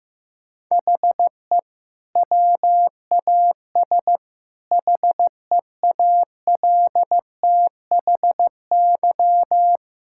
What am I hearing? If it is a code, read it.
Morse code, heard as a keyed tone, about 15 wpm, 700 Hz. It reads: HE WAS HEALTHY
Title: Sentences from Top 1000 Words - Review Part 6 - 15wpm